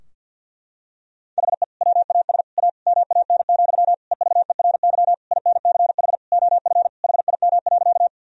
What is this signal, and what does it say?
Morse code, heard as a keyed tone, about 50 wpm, 700 Hz. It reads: HE WAS A MAN, EVERY INCH OF HIM.
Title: Sentences from Top 1000 Words - Review Part 3 - 50wpm